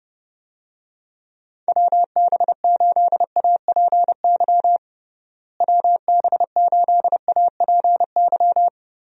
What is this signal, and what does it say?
Morse code, heard as a keyed tone, about 30 wpm, 700 Hz. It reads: WB8APY WB8APY
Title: USA Call Signs - 6 Characters - Repeated 2x - 30wpm